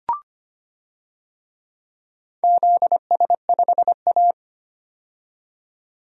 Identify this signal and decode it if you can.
Morse code, heard as a keyed tone, about 25 wpm, 700 Hz. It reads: ZS5A
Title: DXCC-100 Call Signs - 4 Characters - 25wpm